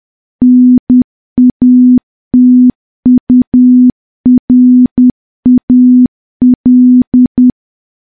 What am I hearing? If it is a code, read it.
Morse code, heard as a keyed tone, about 10 wpm, 250 Hz. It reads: NATURAL
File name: morse / morse-training-10WPM